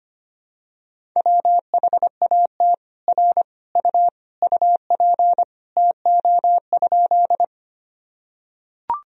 Morse code, heard as a keyed tone, about 25 wpm, 700 Hz. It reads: WHAT R U UP TO?
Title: Sentences from Top 100 Words and Abbreviations 25wpm